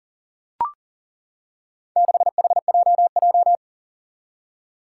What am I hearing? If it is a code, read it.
Morse code, heard as a keyed tone, about 37 wpm, 700 Hz. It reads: 6HJJ
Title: Four Challenging Letter-Number Combinations 40wpm